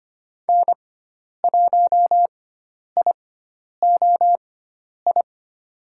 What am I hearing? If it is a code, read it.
Morse code, heard as a keyed tone, about 25 wpm, spaced as for 10 wpm, 700 Hz. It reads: N1IOI
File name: Lesson 16g - Letter c - Call Signs 25wpm